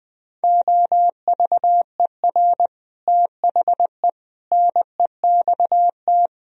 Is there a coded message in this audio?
Yes: Morse code from 0.4 to 6.3 s, heard as a keyed tone, about 21 wpm, 700 Hz. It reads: OVER THE NEXT